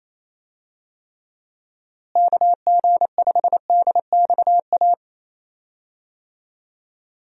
Morse code, heard as a keyed tone, about 28 wpm, 700 Hz. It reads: KG5DXA
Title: USA Call Signs - 6 Characters - Repeated 2x - 28wpm